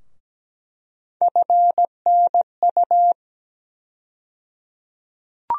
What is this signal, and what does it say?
Morse code, heard as a keyed tone, about 17 wpm, 700 Hz. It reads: FNU
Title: USA Call Sign Postfixes - 3 Letters 17wpm